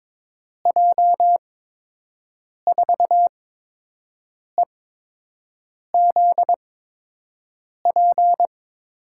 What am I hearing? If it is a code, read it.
Morse code, heard as a keyed tone, about 22 wpm, spaced as for 6 wpm, 700 Hz. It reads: J4EZP
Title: USA Call Signs - 8x Character Spacing - 22wpm